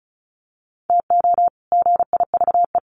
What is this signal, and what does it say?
Morse code, heard as a keyed tone, about 34 wpm, 700 Hz. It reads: TO GIVE
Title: Sets of 2 Words 35wpm